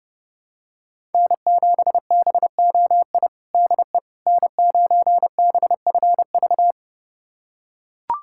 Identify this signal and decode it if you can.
Morse code, heard as a keyed tone, about 30 wpm, 700 Hz. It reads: N7BOI DE N9BFV
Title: USA Callsign DE Callsign - 5 Characters - 30wpm